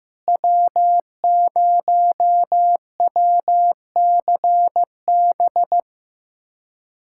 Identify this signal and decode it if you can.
Morse code, heard as a keyed tone, about 15 wpm, 700 Hz. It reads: W0WCB
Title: USA Callsign DE Callsign 15wpm